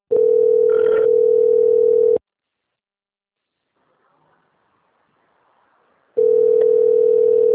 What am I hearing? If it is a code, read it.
Ringback tone (North America), 440 + 480 Hz.